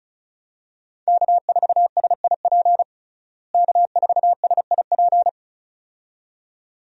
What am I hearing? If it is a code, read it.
Morse code, heard as a keyed tone, about 34 wpm, 700 Hz. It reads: K4SIP K4SIP
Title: USA Call Signs - 5 Characters - Repeated 2x - 35wpm